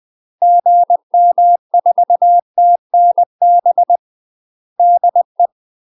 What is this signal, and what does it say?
Morse code, heard as a keyed tone, about 20 wpm, 700 Hz. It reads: GM4TNB DE
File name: UK MASTER QSOs TRAINING / 20WPM / UK MASTER QSOs 0025